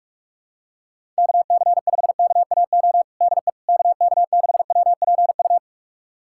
Morse code, heard as a keyed tone, about 46 wpm, 700 Hz. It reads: KC5KAO DE KK6WPU
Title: USA Callsign DE Callsign 45wpm